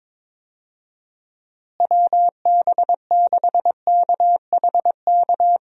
Morse code, heard as a keyed tone, about 22 wpm, 700 Hz. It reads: WB6KHK
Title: USA Call Signs - 6 Characters - Repeated 3x - 22wpm